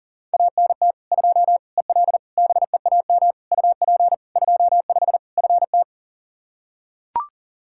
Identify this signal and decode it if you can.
Morse code, heard as a keyed tone, about 40 wpm, 700 Hz. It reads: ANT 2 EL BEAM UP 25 FT
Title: Practice QSO - Exchange 2 - Info - 40wpm